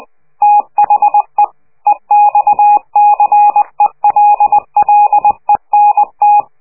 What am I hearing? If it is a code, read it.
Morse code, heard as a keyed tone, about 20 wpm, 830 Hz. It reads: THE EXCELLENT
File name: worldcoastal / scheveningenradio / morsepch